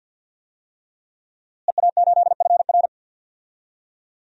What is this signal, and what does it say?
Morse code, heard as a keyed tone, about 51 wpm, 700 Hz. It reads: EA8FR
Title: DXCC-100 Call Signs - 5 Characters - Repeated 3x - 50wpm